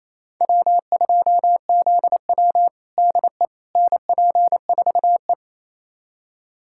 Morse code, heard as a keyed tone, about 27 wpm, 690 Hz. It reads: W2ZW DE NP4E
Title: USA Callsign DE Callsign - 4 Characters - 28wpm